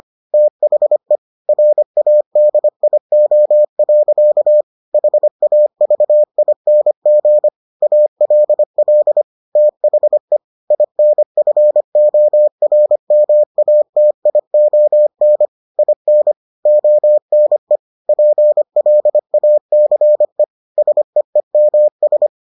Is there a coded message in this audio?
Yes: Morse code from 0.3 to 22.3 s, heard as a keyed tone, about 25 wpm, 600 Hz. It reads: THE RADIO. HAVING ALL THE INFORMATION IN ONE PLACE SEEMS